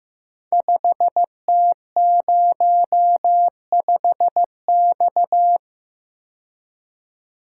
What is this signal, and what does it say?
Morse code, heard as a keyed tone, about 15 wpm, 700 Hz. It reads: 5T05X
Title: Code Groups - 5 Letter:Numbers - 15wpm